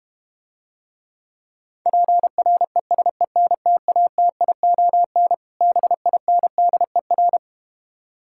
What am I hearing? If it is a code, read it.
Morse code, heard as a keyed tone, about 32 wpm, 700 Hz. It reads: PRESENTATION BINDER